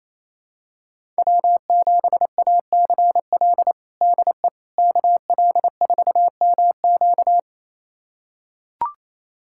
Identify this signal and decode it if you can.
Morse code, heard as a keyed tone, about 28 wpm, 700 Hz. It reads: W7ACL DE KL4MQ
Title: USA Callsign DE Callsign - 5 Characters - 28wpm